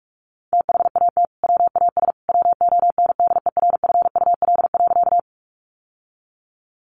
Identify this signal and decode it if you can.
Morse code, heard as a keyed tone, about 44 wpm, 700 Hz. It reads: THAT WAS WONDERFUL.
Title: Sentences from Top 1100 Words 45wpm